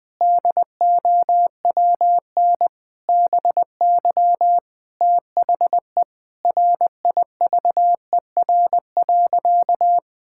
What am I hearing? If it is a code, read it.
Morse code, heard as a keyed tone, about 20 wpm, 700 Hz. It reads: DOWN BY THE RIVER.
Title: Sentences from Top 1000 Words - Review Part 5 - 20wpm